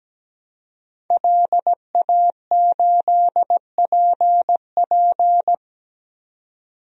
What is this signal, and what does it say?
Morse code, heard as a keyed tone, about 17 wpm, 700 Hz. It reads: LA8PP
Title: DXCC-100 Call Signs - 5 Characters - 17wpm